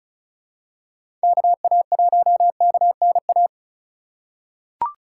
Morse code, heard as a keyed tone, about 36 wpm, 690 Hz. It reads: KA1KNA
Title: USA Call Signs - 6 Characters - 4x Character Spacing - 35wpm